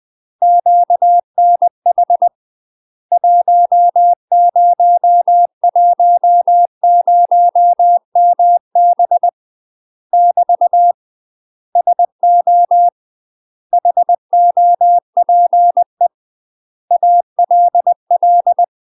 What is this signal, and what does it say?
Morse code, heard as a keyed tone, about 20 wpm, 700 Hz. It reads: QNH 1010MB <BT> SO HOPE ALL
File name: UK MASTER QSOs TRAINING / 20WPM / UK MASTER QSOs 0039